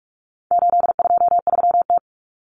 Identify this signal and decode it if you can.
Morse code, heard as a keyed tone, about 44 wpm, 700 Hz. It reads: 823T
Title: Common Cut Numbers - 4-Digit 45wpm